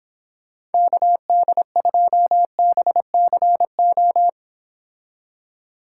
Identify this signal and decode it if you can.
Morse code, heard as a keyed tone, about 26 wpm, 700 Hz. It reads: KD2BCO